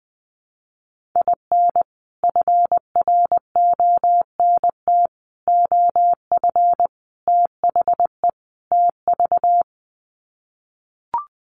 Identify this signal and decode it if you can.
Morse code, heard as a keyed tone, about 20 wpm, 700 Hz. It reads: IN FRONT OF THE TV
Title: Sets of 5 Words Encore2 20wpm